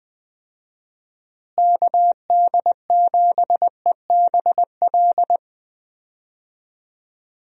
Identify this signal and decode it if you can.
Morse code, heard as a keyed tone, about 20 wpm, 700 Hz. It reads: KD7EBL
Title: USA Call Signs - 6 Characters - Repeated 3x - 20wpm